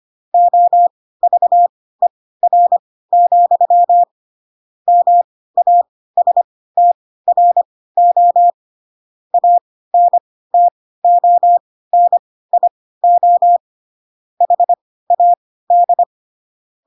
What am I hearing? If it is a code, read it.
Morse code, heard as a keyed tone, about 25 wpm, spaced as for 16 wpm, 700 Hz. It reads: OVER, MASTRO ANTONIO HAD
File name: Pinocchio / Pinocchio ch2 10 25 ik1hge